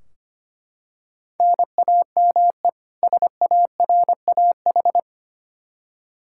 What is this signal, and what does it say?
Morse code, heard as a keyed tone, about 25 wpm, 700 Hz. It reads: NAME SARAH
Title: K1USN SST QSO Practice 25wpm